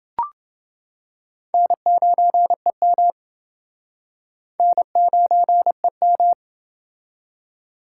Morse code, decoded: N9EM N9EM

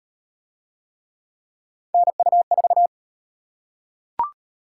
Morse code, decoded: NU4

